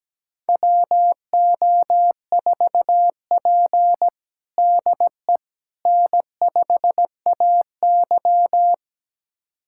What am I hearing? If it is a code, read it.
Morse code, heard as a keyed tone, about 17 wpm, 700 Hz. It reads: WO4P DE N5AY
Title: USA Callsign DE Callsign - 4 Characters - 17wpm